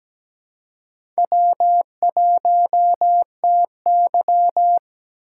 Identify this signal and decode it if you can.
Morse code, heard as a keyed tone, about 17 wpm, 700 Hz. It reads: W1TY
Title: USA Call Signs - 4 Characters - 2x Character Spacing - 17wpm